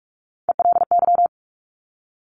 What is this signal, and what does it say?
Morse code, heard as a keyed tone, about 46 wpm, 690 Hz. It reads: ELY
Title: Top Three Letter Combinations in Words 45wpm